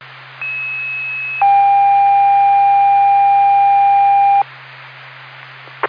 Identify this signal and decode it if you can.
Two-tone page, heard as a one-second tone, then a three-second tone, about 2600 then 800 Hz. It is heard from 0.4 to 4.4 s.
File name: Hopatcong FD